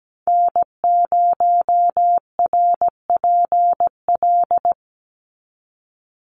Morse code, heard as a keyed tone, about 17 wpm, 700 Hz. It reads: N0RPL
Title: USA Call Signs 17wpm